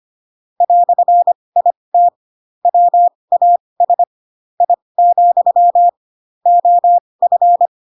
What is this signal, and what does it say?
Morse code, heard as a keyed tone, about 25 wpm, spaced as for 20 wpm, 700 Hz. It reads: "IT WAS I, OF
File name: Pinocchio / Pinocchio ch2 15 25 ik1hge